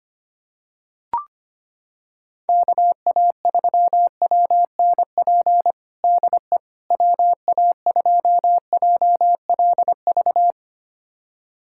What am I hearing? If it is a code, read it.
Morse code, heard as a keyed tone, about 25 wpm, 700 Hz. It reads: KA3WNP DE WA2JLV
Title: USA Callsign DE Callsign - 6 Characters - 25wpm